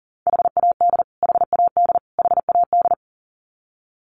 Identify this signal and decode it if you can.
Morse code, heard as a keyed tone, about 40 wpm, 700 Hz. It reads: HAD HAD HAD
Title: Top 500 Words Repeated 40wpm